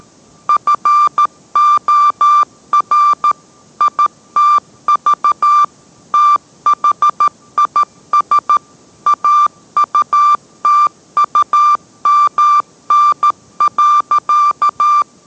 Morse code, heard as a keyed tone, about 13 wpm, 1180 Hz. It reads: FOR ITV THIS AUTUMN.